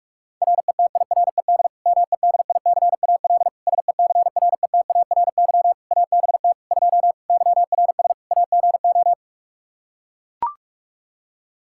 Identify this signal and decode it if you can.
Morse code, heard as a keyed tone, about 41 wpm, 700 Hz. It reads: RETIRED MEDICAL SECRETARY ABT 2 YRS AGO
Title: Practice QSO - Exchange 3 - More Info - 45wpm